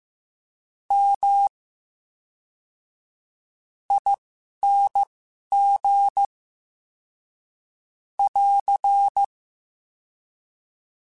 Morse code, heard as a keyed tone, about 15 wpm, spaced as for 6 wpm, 800 Hz. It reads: MING <AR>
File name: Special INGs / Special INGs-mp3s / VerySlowToFaster Ing @10